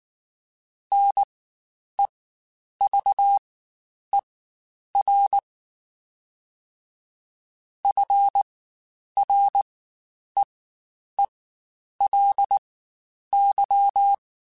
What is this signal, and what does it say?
Morse code, heard as a keyed tone, about 19 wpm, spaced as for 9 wpm, 800 Hz. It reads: NEVER FREELY
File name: Quotes Spaced / Quotes Spaced-mp3s / Advice sp @18